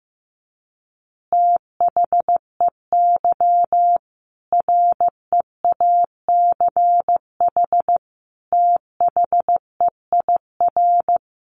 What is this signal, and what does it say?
Morse code, heard as a keyed tone, about 15 wpm, 700 Hz. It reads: THEY REACH THEIR
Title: Easy Phrases 15wpm